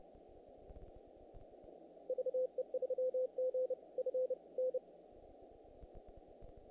Morse code, heard as a keyed tone, about 31 wpm, spaced as for 26 wpm, 510 Hz. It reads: VE3GFN
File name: contests / naqp-cw-k5tr-op-k5pi- -1800